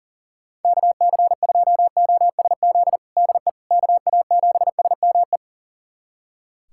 Morse code, heard as a keyed tone, about 37 wpm, 700 Hz. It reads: KC2OSZ DE KA7SME
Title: USA Callsign DE Callsign - 6 Characters - 40wpm